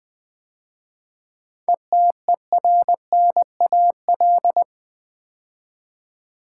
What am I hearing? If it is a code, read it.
Morse code, heard as a keyed tone, about 20 wpm, 700 Hz. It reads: ETERNAL
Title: 7 Letter Words - First 3 Letters Given - 20wpm